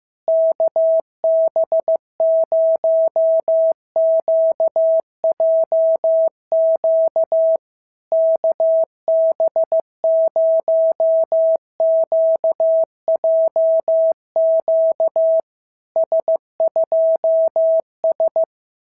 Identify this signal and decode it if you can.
Morse code, heard as a keyed tone, about 15 wpm, 650 Hz. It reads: KB0QJQ KB0QJQ S2S
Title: SOTA QSO Practice 15wpm